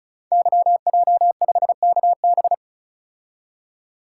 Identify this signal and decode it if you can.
Morse code, heard as a keyed tone, about 36 wpm, 700 Hz. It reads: YJ5KB
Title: Code Groups - 5 Letter:Numbers - 35wpm